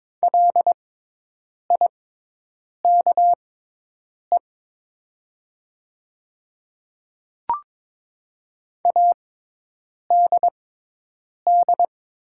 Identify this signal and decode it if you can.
Morse code, heard as a keyed tone, about 22 wpm, spaced as for 8 wpm, 700 Hz. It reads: LIKE ADD
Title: 6x Character Spacing - Top 200 Words - 22wpm